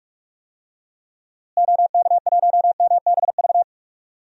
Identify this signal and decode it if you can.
Morse code, heard as a keyed tone, about 41 wpm, 700 Hz. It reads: OK1MBV
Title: DXCC-100 Call Signs - 6 Characters - Repeated 3x - 45wpm